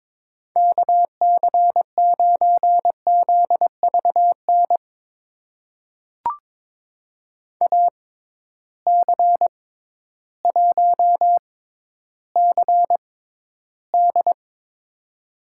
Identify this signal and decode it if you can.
Morse code, heard as a keyed tone, about 22 wpm, spaced as for 12 wpm, 700 Hz. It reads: KC9ZVN AC1CD
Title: USA Call Signs - 6x Character Spacing - 22wpm